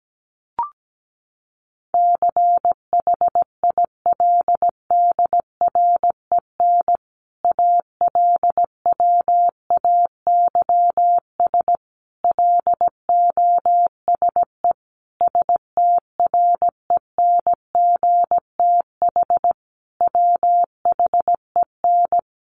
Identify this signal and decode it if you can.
Morse code, heard as a keyed tone, about 17 wpm, 700 Hz. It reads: CHILDREN ALWAYS LOSE STRENGTH WHEN
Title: Sentences from Top 1300 Words 17wpm